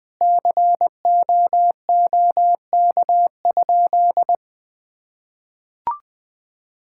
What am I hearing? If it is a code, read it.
Morse code, heard as a keyed tone, about 20 wpm, 700 Hz. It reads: COOK?
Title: Sentences with 6-Word Phrases - 20wpm